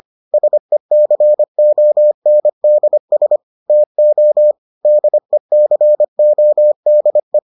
Morse code, heard as a keyed tone, about 25 wpm, 600 Hz. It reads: SECONDS TO DECODE